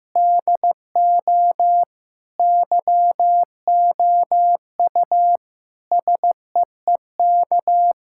Morse code, heard as a keyed tone, about 16 wpm, 700 Hz. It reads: DO YOU SEEK
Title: Sentences from Top 1600 Words 15wpm